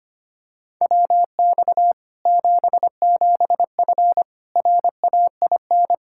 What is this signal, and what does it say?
Morse code, heard as a keyed tone, about 25 wpm, 700 Hz. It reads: WX 77F RAIN
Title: Practice QSO - Exchange 2 - Info - 25wpm